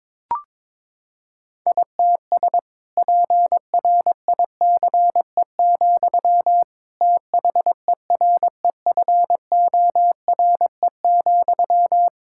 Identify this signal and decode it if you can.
Morse code, heard as a keyed tone, about 22 wpm, 700 Hz. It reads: ITS PRICE, THEREFORE,
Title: Sentences from Top 1000 Words - Review Part 5 - 22wpm